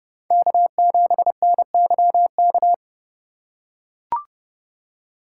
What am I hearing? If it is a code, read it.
Morse code, heard as a keyed tone, about 30 wpm, 700 Hz. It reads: K7NYK
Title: USA Call Signs - 5 Characters - 4x Character Spacing - 30wpm